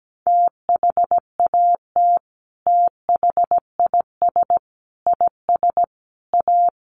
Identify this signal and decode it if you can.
Morse code, heard as a keyed tone, about 17 wpm, 700 Hz. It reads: THAT THIS IS A
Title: Sets of 4 Words 17wpm